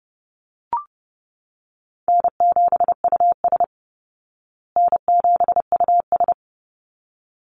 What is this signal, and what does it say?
Morse code, heard as a keyed tone, about 30 wpm, 700 Hz. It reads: N7US N7US
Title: Call Signs Repeated 3x - 30wpm